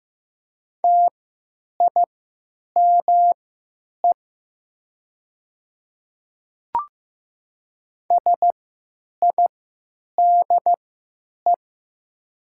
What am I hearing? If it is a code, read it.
Morse code, heard as a keyed tone, about 15 wpm, spaced as for 9 wpm, 700 Hz. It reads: TIME SIDE
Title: 3x Character Spacing - Top 500 Words - 15wpm